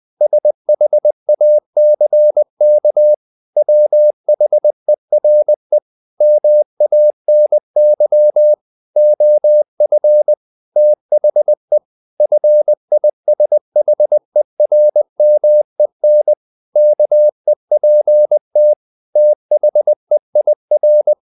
Morse code, heard as a keyed tone, about 20 wpm, 600 Hz. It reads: SHACK WHERE MANY OF THE FISHERMEN KEPT THEIR
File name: the old man and the sea / the old man and the sea part 1 20wpm